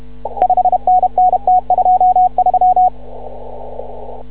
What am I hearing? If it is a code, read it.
Morse code, heard as a keyed tone, about 31 wpm, 740 Hz. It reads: 5NN T23
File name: CW / ha6oi 2